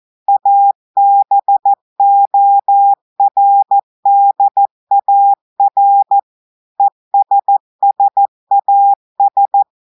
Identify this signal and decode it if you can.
Morse code, heard as a keyed tone, about 14 wpm, 800 Hz. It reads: ABORDAR ESSAS
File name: audios / portugais / portugais texte en clair 2 vitesse 14 mots minutes